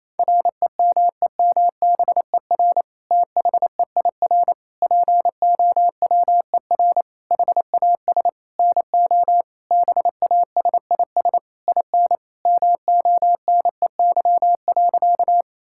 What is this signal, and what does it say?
Morse code, heard as a keyed tone, about 27 wpm, 700 Hz. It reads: REMEMBER THEIR POWER HAS NO BASIS IN MONEY.
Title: Sentences from Top 1000 Words - Review Part 5 - 28wpm